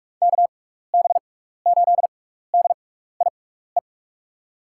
Morse code, heard as a keyed tone, about 44 wpm, spaced as for 15 wpm, 700 Hz. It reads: KB8DIE
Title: USA Call Signs - 6 Characters - 6x Character Spacing - 45wpm